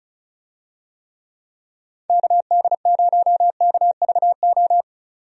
Morse code, heard as a keyed tone, about 36 wpm, 700 Hz. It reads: KD0KVO
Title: USA Call Signs - 6 Characters - 35wpm